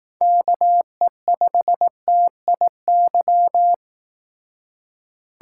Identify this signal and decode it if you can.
Morse code, heard as a keyed tone, about 18 wpm, 700 Hz. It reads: KE5TIY
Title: Speed Racing - USA Call Signs 15wpm